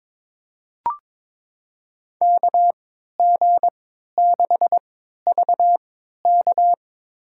Morse code, heard as a keyed tone, about 22 wpm, spaced as for 13 wpm, 700 Hz. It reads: KG6VK
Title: USA Call Signs - 3x Character Spacing - 22wpm